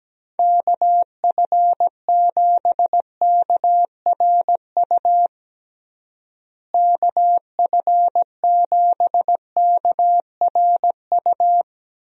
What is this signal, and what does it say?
Morse code, heard as a keyed tone, about 17 wpm, 700 Hz. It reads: KF7KRU KF7KRU
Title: USA Call Signs - 6 Characters - Repeated 3x - 17wpm